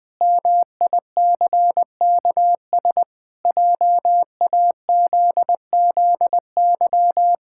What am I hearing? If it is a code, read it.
Morse code, heard as a keyed tone, about 21 wpm, 700 Hz. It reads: MICKS JAZZY
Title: Pangram Challenge v2 - 20wpm